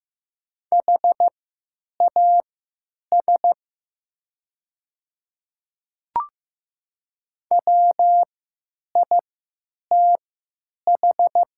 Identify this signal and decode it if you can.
Morse code, heard as a keyed tone, about 15 wpm, spaced as for 9 wpm, 700 Hz. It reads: HAS WITH
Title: 3x Character Spacing - Top 100 Words - 15wpm